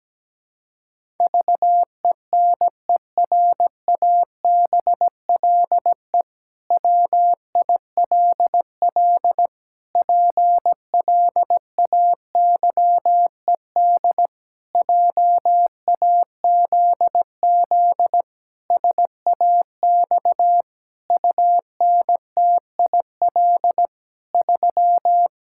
Morse code, heard as a keyed tone, about 17 wpm, 700 Hz. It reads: VENERABLE WILL PLAYED JAZZ SAX UNTIL 3
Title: Pangram Challenge v2 - 17wpm